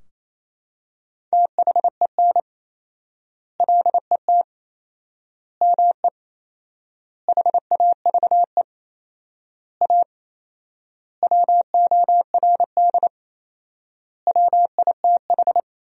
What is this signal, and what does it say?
Morse code, heard as a keyed tone, about 28 wpm, 700 Hz. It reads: THEN LET ME HAVE A WORD WITH
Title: 4x Word-Spacing - Sentences from Top 300 Words - 28wpm